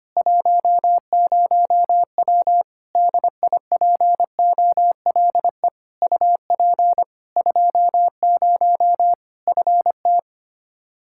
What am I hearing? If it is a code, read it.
Morse code, heard as a keyed tone, about 25 wpm, 700 Hz. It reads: 10W DIPOLE UP 20 FT
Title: Practice QSO - Exchange 2 - Info - 25wpm